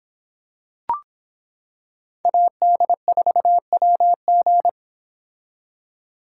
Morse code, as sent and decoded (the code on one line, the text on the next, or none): .- -.. ....- .-- --.
AD4WG